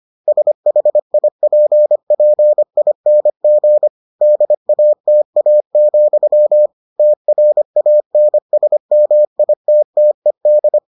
Morse code, heard as a keyed tone, about 25 wpm, 600 Hz. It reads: SHIPPING DATA, TRANSMITTED